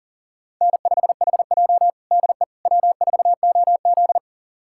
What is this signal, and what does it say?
Morse code, heard as a keyed tone, about 40 wpm, 700 Hz. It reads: N5HJ DE W4OZ
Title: USA Callsign DE Callsign - 4 Characters - 40wpm